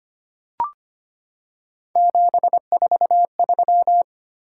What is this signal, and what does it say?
Morse code, heard as a keyed tone, about 25 wpm, 700 Hz. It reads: 743
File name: Three Letter-Number Combinations - 25wpm